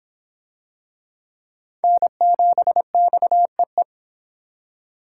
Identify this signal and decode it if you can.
Morse code, heard as a keyed tone, about 24 wpm, 700 Hz. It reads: N7XEE